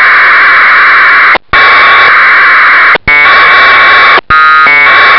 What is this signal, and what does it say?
Packet radio (AX.25, as in APRS).